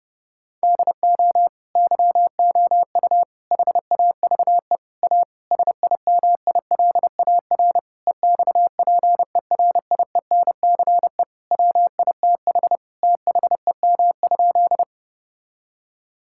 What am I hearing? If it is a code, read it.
Morse code, heard as a keyed tone, about 30 wpm, 700 Hz. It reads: DO YOU HAVE A SIMILAR EXPERIENCE WITH THEM?
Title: Sentences from Top 1000 Words - Review Part 5 - 30wpm